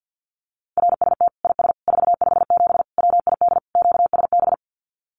Morse code, heard as a keyed tone, about 50 wpm, 700 Hz. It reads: RST IS 457 WID QSB
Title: Common QSO Phrases 50wpm